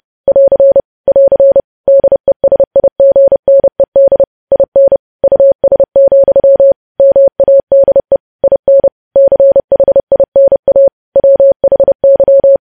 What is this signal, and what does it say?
Morse code, heard as a keyed tone, about 30 wpm, 550 Hz. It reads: <AR> <AR> DESIGNED IN US, MADE IN CHINA WHY
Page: news headlines in morse code at 30 wpm